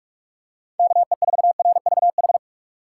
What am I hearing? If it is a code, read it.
Morse code, heard as a keyed tone, about 44 wpm, 700 Hz. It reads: KE4RVH